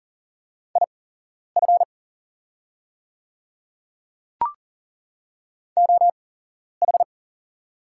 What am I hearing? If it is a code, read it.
Morse code, heard as a keyed tone, about 40 wpm, spaced as for 11 wpm, 690 Hz. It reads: IF OH
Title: 8x Character Spacing - Top 500 Words - 40wpm